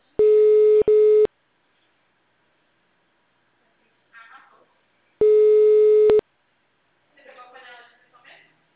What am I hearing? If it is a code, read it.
Ringback tone (Europe, 425 Hz), with speech.